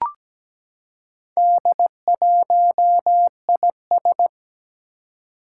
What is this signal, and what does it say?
Morse code, heard as a keyed tone, about 17 wpm, 700 Hz. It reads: D1IS
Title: Four Challenging Letter-Number Combinations 17wpm